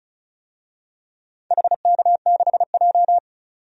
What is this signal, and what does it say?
Morse code, heard as a keyed tone, about 36 wpm, 700 Hz. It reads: HK6J
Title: DXCC-100 Call Signs - 4 Characters - 35wpm